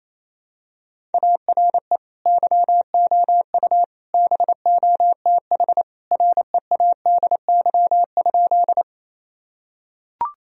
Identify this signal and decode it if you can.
Morse code, heard as a keyed tone, about 28 wpm, 700 Hz. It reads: ARE YOU BOTH READY?
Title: Sentences from Top 1000 Words - Review Part 1 - 28wpm